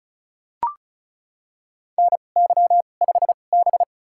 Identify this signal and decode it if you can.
Morse code, heard as a keyed tone, about 36 wpm, spaced as for 25 wpm, 700 Hz. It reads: NY5B
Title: USA Call Signs - 4 Characters - 2x Character Spacing - 35wpm